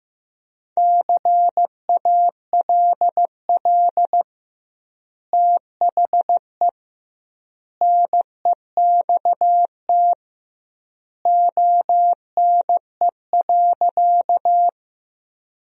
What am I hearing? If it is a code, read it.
Morse code, heard as a keyed tone, about 15 wpm, 700 Hz. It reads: CALL THE NEXT ONE.
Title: 2x Word-Spacing - Sentences from Top 300 Words - 15wpm